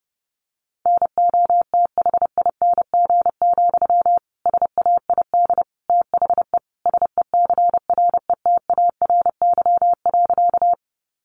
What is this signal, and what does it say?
Morse code, heard as a keyed tone, about 30 wpm, 700 Hz. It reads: NOTHING, SAID THE SECRETARY.
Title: Sentences from Top 1300 Words 30wpm